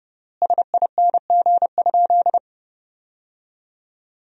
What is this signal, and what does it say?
Morse code, heard as a keyed tone, about 30 wpm, 700 Hz. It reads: SING?
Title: 3x Word-Spacing - Sentences from Top 400 Words - 30wpm